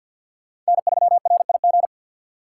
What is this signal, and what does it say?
Morse code, heard as a keyed tone, about 61 wpm, 700 Hz. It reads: N3RIG
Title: Speed Racing - USA 1x3 Call Signs 50wpm